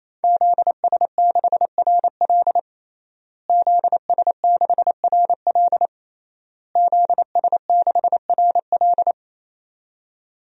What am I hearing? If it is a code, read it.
Morse code, heard as a keyed tone, about 28 wpm, 700 Hz. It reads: ZS6RL ZS6RL ZS6RL
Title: DXCC-100 Call Signs - 5 Characters - Repeated 3x - 28wpm